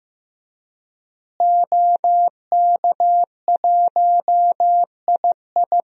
Morse code, heard as a keyed tone, about 16 wpm, 700 Hz. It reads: OK1II
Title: DXCC-100 Call Signs - 5 Characters - Repeated 3x - 15wpm